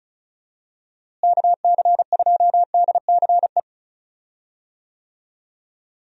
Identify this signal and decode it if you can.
Morse code, heard as a keyed tone, about 36 wpm, 700 Hz. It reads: KC2DCE
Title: USA Call Signs - 6 Characters - Repeated 3x - 35wpm